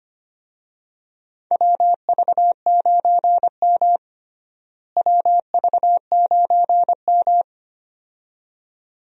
Morse code, heard as a keyed tone, about 25 wpm, 700 Hz. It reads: WV9M WV9M
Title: USA Call Signs - 4 Characters - Repeated 2x - 25wpm